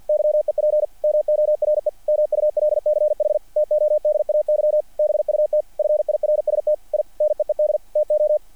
Morse code, heard as a keyed tone, about 48 wpm, 660 Hz. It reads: YEP MORE MULCH TODAY BUT FIRST I NEED TO
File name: archived / CRSnet